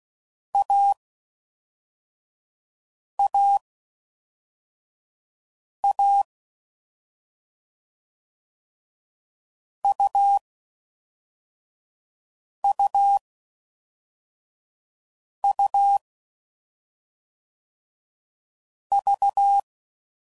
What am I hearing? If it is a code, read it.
Morse code, heard as a keyed tone, about 16 wpm, 800 Hz. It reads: A A A U U U V
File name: Combinations / Difficult Combos / AUV Short